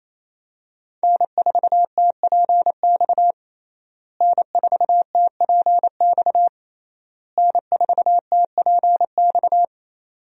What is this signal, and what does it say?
Morse code, heard as a keyed tone, about 27 wpm, 700 Hz. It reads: N4TPX N4TPX N4TPX
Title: USA Call Signs - 5 Characters - Repeated 3x - 28wpm